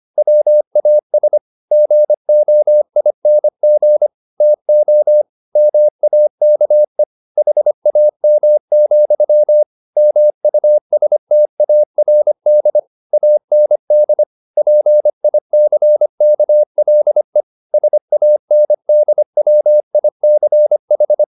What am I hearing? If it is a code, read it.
Morse code, heard as a keyed tone, about 25 wpm, 600 Hz. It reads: WAS GOING TO MAKE HAM, MUSTARD AND PICKLE SANDWICH